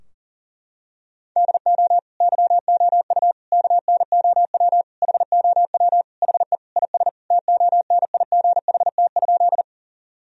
Morse code, heard as a keyed tone, about 37 wpm, 700 Hz. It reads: DO YOU KNOW HOW HE IS TONIGHT?
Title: Sentences from Top 1000 Words - Review Part 3 - 40wpm